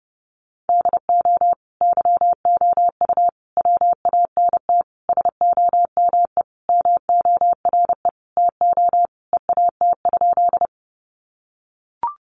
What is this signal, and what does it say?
Morse code, heard as a keyed tone, about 30 wpm, 700 Hz. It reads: DO YOU WANT SOME MORE TO EAT?
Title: Sentences from Top 300 Words 30wpm